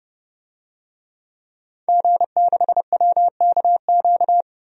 Morse code, heard as a keyed tone, about 30 wpm, 700 Hz. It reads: G6WKQ
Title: DXCC-100 Call Signs - 5 Characters - 30wpm